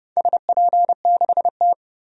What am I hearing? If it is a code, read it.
Morse code, heard as a keyed tone, about 29 wpm, 700 Hz. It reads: SP6T
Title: DXCC-100 Call Signs - 4 Characters - 30wpm